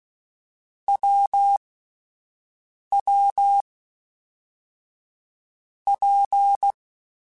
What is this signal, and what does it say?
Morse code, heard as a keyed tone, about 16 wpm, 800 Hz. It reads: W W P
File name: Difficult Combos / Difficult Combos-mp3s / WPJ WPJ1 @10